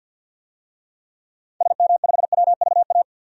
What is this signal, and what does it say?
Morse code, heard as a keyed tone, about 54 wpm, 700 Hz. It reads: SM5WVA
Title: DXCC-100 Call Signs - 6 Characters - 50wpm